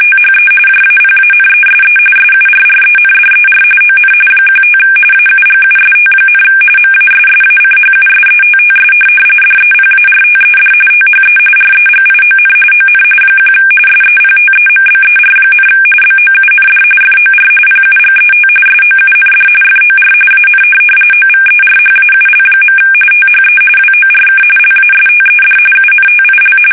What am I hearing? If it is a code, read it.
RTTY, about 45.45 baud. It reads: THE JAPANESE SURRENDER ON 15 AUGUST, MASSACHUSETTS DEPARTED THE REGION ON 1 SEPTEMBER, BOUND FOR PUGET SOUND FOR AN OVERHAUL. THE WORK LASTED UNTIL
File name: fsk850 / USS-Massachusetts